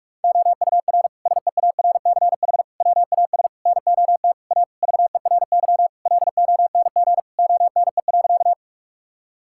Morse code, heard as a keyed tone, about 41 wpm, 700 Hz. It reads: OUR SEARCH WAS NOT A VERY LONG ONE.
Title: Sentences from Top 1000 Words - Review Part 5 - 45wpm